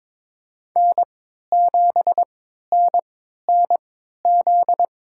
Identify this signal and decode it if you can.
Morse code, heard as a keyed tone, about 22 wpm, spaced as for 13 wpm, 700 Hz. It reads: N7NNZ
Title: USA Call Signs - 5 Characters - 3x Character Spacing - 22wpm